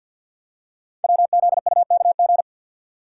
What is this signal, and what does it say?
Morse code, heard as a keyed tone, about 51 wpm, 700 Hz. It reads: WZUKG
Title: Code Groups - 5 Letters - 50wpm